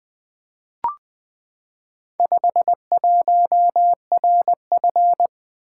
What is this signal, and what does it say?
Morse code, heard as a keyed tone, about 20 wpm, 700 Hz. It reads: 51RF
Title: Four Letter-Number Combinations - 20wpm